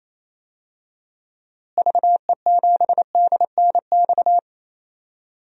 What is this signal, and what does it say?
Morse code, heard as a keyed tone, about 28 wpm, 700 Hz. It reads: VE7DNX
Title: DXCC-100 Call Signs - 6 Characters - Repeated 2x - 28wpm